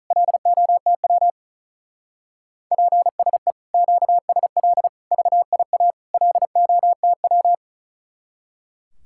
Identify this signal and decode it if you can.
Morse code, heard as a keyed tone, about 43 wpm, 700 Hz. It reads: LOTW PSE QSL VIA LOTW
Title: Speed-Racing Common QSO Phrases - 35wpm